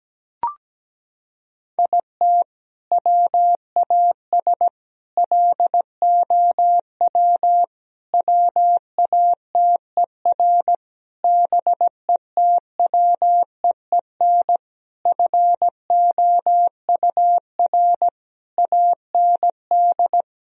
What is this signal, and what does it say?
Morse code, heard as a keyed tone, about 17 wpm, 700 Hz. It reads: IT WAS LOW WATER BETWEEN FOUR AND
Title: Sentences from Top 1000 Words - Review Part 2 - 17wpm